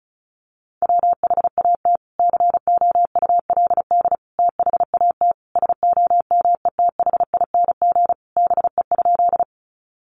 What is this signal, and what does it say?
Morse code, heard as a keyed tone, about 34 wpm, 700 Hz. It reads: WHAT COULD THAT SOMETHING BE?
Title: Sentences from Top 600 Words 35wpm